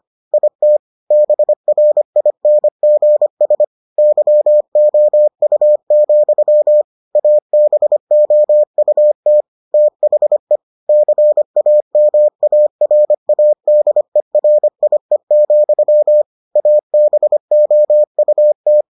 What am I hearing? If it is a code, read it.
Morse code, heard as a keyed tone, about 25 wpm, 600 Hz. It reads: IT BRINGS YOU, ABOUT THE CAMARADERIE, ABOUT